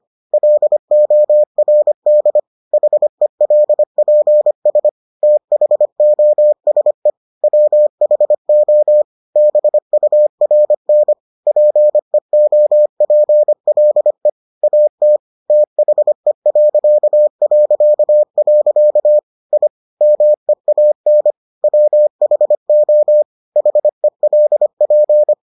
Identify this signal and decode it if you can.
Morse code, heard as a keyed tone, about 25 wpm, 600 Hz. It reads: LORD HELPS THOSE WHO BURN PEOPLE AT THE... I MEAN WHO HELP